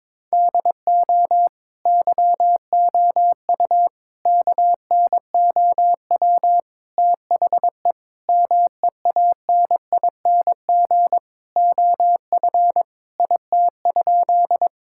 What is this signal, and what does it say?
Morse code, heard as a keyed tone, about 22 wpm, 700 Hz. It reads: DO YOU KNOW THE MEANING OF IT?
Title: Sentences from Top 1000 Words - Review Part 6 - 22wpm